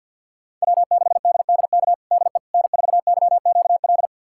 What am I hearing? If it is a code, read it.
Morse code, heard as a keyed tone, about 52 wpm, 700 Hz. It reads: W6DDK DE N4YQL
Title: USA Callsign DE Callsign - 5 Characters - 50wpm